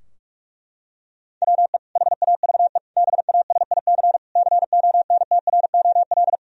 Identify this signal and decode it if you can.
Morse code, heard as a keyed tone, about 46 wpm, 700 Hz. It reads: WE HAVE BASIC CONTROL
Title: Sentences from Top 1000 Words - Review Part 3 - 45wpm